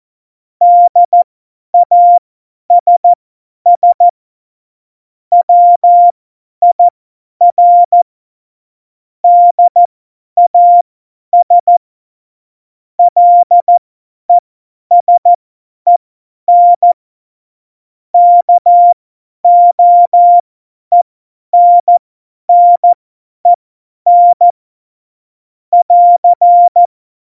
Morse code, heard as a keyed tone, about 14 wpm, spaced as for 10 wpm, 700 Hz. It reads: DASS WIR DAS LESEN KOENNEN <AR>
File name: Morsekurs / Witze / Witz 09. Oktober 2018, 50 F70